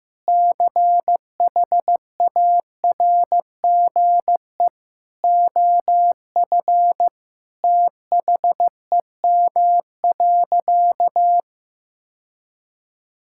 Morse code, heard as a keyed tone, about 15 wpm, 700 Hz. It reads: CHARGE OF THEM.
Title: Sentences from Top 1000 Words - Review Part 6 - 15wpm